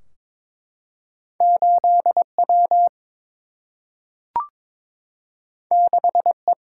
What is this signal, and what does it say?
Morse code, heard as a keyed tone, about 22 wpm, 700 Hz. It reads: 8 W 6 E
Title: Single-Letter Single-Number 22wpm